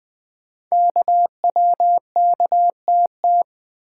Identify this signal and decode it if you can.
Morse code, heard as a keyed tone, about 20 wpm, 700 Hz. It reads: KWKTT
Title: Code Groups - 5 Letters - 20wpm